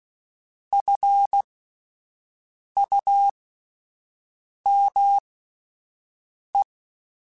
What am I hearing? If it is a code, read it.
Morse code, heard as a keyed tone, about 16 wpm, spaced as for 5 wpm, 800 Hz. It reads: FUME
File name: Alphabetical / Alphabetical mp3s / Fs @5